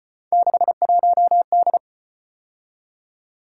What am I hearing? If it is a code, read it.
Morse code, heard as a keyed tone, about 34 wpm, 700 Hz. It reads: 61D